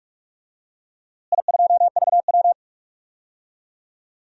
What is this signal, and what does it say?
Morse code, heard as a keyed tone, about 43 wpm, 700 Hz. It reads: I2VW
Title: DXCC-100 Call Signs - 4 Characters - Repeated 3x - 45wpm